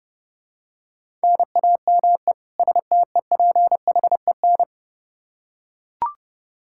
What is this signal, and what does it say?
Morse code, heard as a keyed tone, about 30 wpm, 700 Hz. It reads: NAME STEPHEN
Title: Practice QSO - Exchange 1 - Beginning - 30wpm